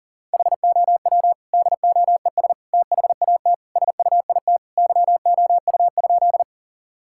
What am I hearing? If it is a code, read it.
Morse code, heard as a keyed tone, about 40 wpm, 700 Hz. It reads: HOW DOES THAT SUIT YOU?
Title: Sentences from Top 1600 Words 40wpm